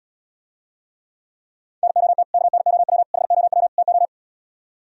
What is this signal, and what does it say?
Morse code, heard as a keyed tone, about 77 wpm, 700 Hz. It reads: NOT KNOW HOW TO